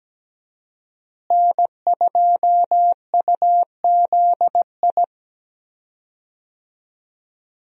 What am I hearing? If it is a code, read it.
Morse code, heard as a keyed tone, about 17 wpm, 700 Hz. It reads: N2UZI
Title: USA Call Signs - 5 Characters - Repeated 2x - 17wpm